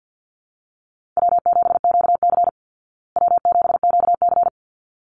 Morse code, heard as a keyed tone, about 48 wpm, 700 Hz. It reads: W7QC W7QC
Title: Call Signs Repeated 3x - 50wpm